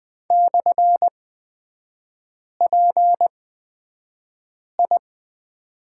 Morse code, heard as a keyed tone, about 20 wpm, spaced as for 5 wpm, 700 Hz. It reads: /PI
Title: 250 Lesson 40e - BK - Three Characters 20wpm